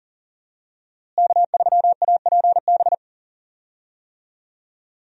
Morse code, heard as a keyed tone, about 37 wpm, 700 Hz. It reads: K3APB
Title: USA Call Signs - 5 Characters - Repeated 2x - 40wpm